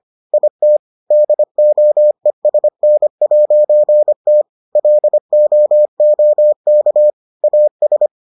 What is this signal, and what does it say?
Morse code, heard as a keyed tone, about 25 wpm, 600 Hz. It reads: IT DOESN'T LOOK AS